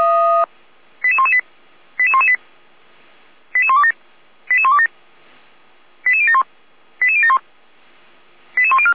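Five-tone selective call (ZVEI), heard as quick runs of five tones, tones of about 70 ms. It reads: 88118 88118 88127 88127 88071 88071 88104
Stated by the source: ZVEI five-tone selective call received via SDR.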